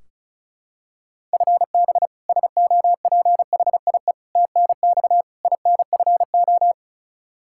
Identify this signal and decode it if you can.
Morse code, heard as a keyed tone, about 36 wpm, 700 Hz. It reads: FB SOPHIE TNX INFO
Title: Practice QSO - Exchange 3 - More Info - 35wpm